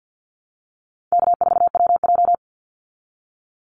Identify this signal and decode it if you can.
Morse code, heard as a keyed tone, about 50 wpm, 700 Hz. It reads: K4WJ
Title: Call Signs Repeated 2x - 50wpm